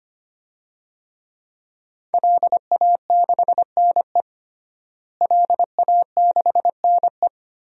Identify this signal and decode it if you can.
Morse code, heard as a keyed tone, about 25 wpm, 700 Hz. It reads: LA6NE LA6NE
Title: DXCC-100 Call Signs - 5 Characters - Repeated 3x - 25wpm